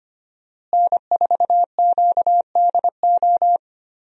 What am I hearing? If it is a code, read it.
Morse code, heard as a keyed tone, about 25 wpm, 700 Hz. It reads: N4QDO
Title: USA Call Signs - 5 Characters - Repeated 2x - 25wpm